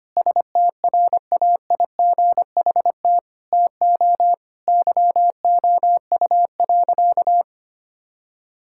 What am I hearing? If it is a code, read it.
Morse code, heard as a keyed tone, about 25 wpm, 700 Hz. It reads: STRAIGHT TO YOU.
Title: Sentences from Top 1000 Words - Review Part 6 - 25wpm